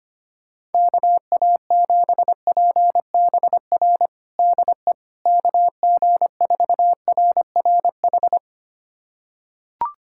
Morse code, heard as a keyed tone, about 25 wpm, 700 Hz. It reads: KA7PBR DE KG4RRH
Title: USA Callsign DE Callsign - 6 Characters - 25wpm